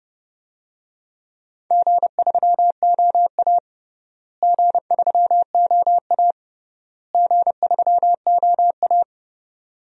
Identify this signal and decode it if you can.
Morse code, heard as a keyed tone, about 31 wpm, 700 Hz. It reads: G3OA G3OA G3OA
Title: DXCC-100 Call Signs - 4 Characters - Repeated 3x - 30wpm